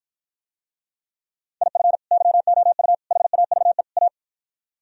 Morse code, heard as a keyed tone, about 53 wpm, 700 Hz. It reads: IF YOU HAVE A